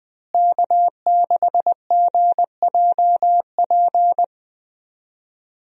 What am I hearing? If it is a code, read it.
Morse code, heard as a keyed tone, about 20 wpm, 700 Hz. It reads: K6GJP
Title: USA Call Signs - 5 Characters - Repeated 2x - 20wpm